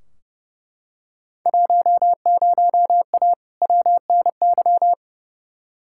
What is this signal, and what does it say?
Morse code, heard as a keyed tone, about 30 wpm, 700 Hz. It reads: 10A WNY
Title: Field Day Exchanges - 30wpm